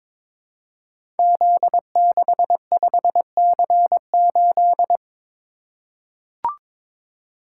Morse code, heard as a keyed tone, about 22 wpm, 700 Hz. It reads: Z65C8
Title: Code Groups - 5 Letter:Numbers - 22wpm